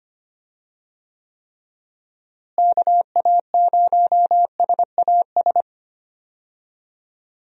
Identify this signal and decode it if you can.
Morse code, heard as a keyed tone, about 25 wpm, 700 Hz. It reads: KA0SAS
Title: USA Call Signs - 6 Characters - Repeated 3x - 25wpm